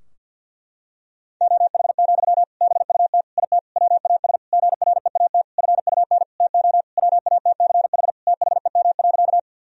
Morse code, heard as a keyed tone, about 52 wpm, 700 Hz. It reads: OH, BUT IT WAS GREAT FUN TO WATCH THEM.
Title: Sentences from Top 1000 Words - Review Part 4 - 50wpm